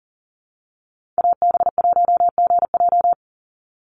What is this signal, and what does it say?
Morse code, heard as a keyed tone, about 40 wpm, 700 Hz. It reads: AB1GJ
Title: Call Signs Repeated 3x - 40wpm